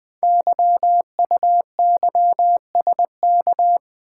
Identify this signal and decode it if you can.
Morse code, heard as a keyed tone, about 20 wpm, 700 Hz. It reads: YUYSK
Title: Code Groups - 5 Letters - 20wpm